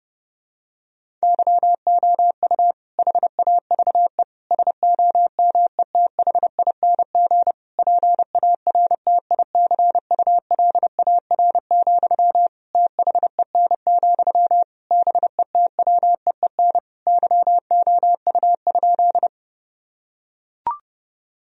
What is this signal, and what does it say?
Morse code, heard as a keyed tone, about 32 wpm, 700 Hz. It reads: YOU HAVE SOMETHING PARTICULAR, THEN, BETWEEN YOU?
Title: Sentences from Top 1000 Words - Review Part 4 - 30wpm